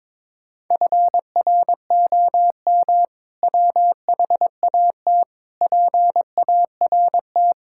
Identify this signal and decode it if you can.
Morse code, heard as a keyed tone, about 22 wpm, 700 Hz. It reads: FROM WHAT PART
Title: Sentences from Top 1000 Words - Review Part 6 - 22wpm